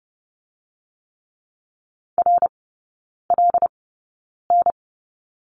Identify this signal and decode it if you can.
Morse code, heard as a keyed tone, about 30 wpm, spaced as for 9 wpm, 700 Hz. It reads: RLN
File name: course / 30:10wpm / 087 Lesson 15e - Number 5 - Three Characters 30wpm